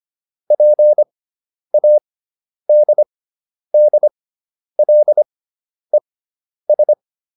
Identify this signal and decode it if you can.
Morse code, heard as a keyed tone, about 25 wpm, spaced as for 10 wpm, 600 Hz. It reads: PADDLES